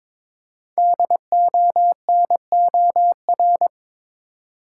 Morse code, heard as a keyed tone, about 22 wpm, 690 Hz. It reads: DONOR
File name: 5 Letter Words - First 3 Letters Given - 22wpm